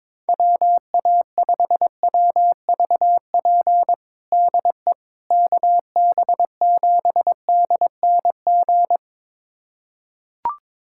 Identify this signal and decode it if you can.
Morse code, heard as a keyed tone, about 22 wpm, 700 Hz. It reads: WA5WVP DE KB7DNG
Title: USA Callsign DE Callsign - 6 Characters - 22wpm